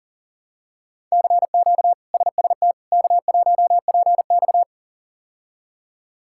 Morse code, heard as a keyed tone, about 40 wpm, 700 Hz. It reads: CQ SST K1PX
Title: K1USN SST QSO Practice - 40wpm